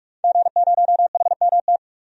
Morse code, heard as a keyed tone, about 42 wpm, 700 Hz. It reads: G0HMT
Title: DXCC-100 Call Signs - 5 Characters - 45wpm